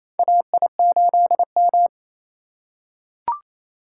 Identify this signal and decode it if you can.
Morse code, heard as a keyed tone, about 28 wpm, 700 Hz. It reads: AI8M
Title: USA Call Signs - 3x Character Spacing - 28wpm